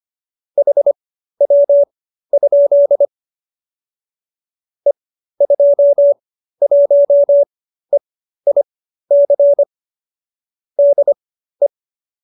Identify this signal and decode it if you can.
Morse code, heard as a keyed tone, about 25 wpm, spaced as for 13 wpm, 570 Hz. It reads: HW? E21EIC DE
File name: qso208b 13